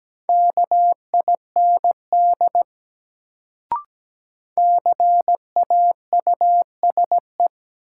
Morse code, heard as a keyed tone, about 17 wpm, 700 Hz. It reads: KIND CAUSE
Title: Sentences from Top 200 Words - New Word Drill - 17wpm